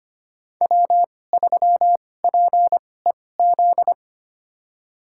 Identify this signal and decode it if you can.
Morse code, heard as a keyed tone, about 25 wpm, spaced as for 18 wpm, 700 Hz. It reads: W3PEZ
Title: USA Call Signs - 5 Characters - 2x Character Spacing - 25wpm